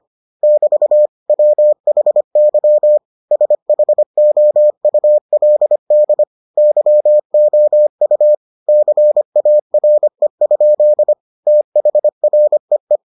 Morse code, heard as a keyed tone, about 25 wpm, 600 Hz. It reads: <BT> WHY SHOULD YOU CARE? THREE